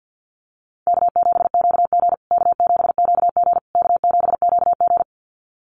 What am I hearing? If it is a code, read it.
Morse code, heard as a keyed tone, about 50 wpm, 700 Hz. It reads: K7QG K7QG K7QG
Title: USA Call Signs Repeated 3x 50wpm